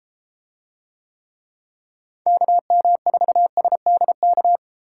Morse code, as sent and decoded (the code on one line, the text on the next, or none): -.- -- ....- ... -.. -.-
KM4SDK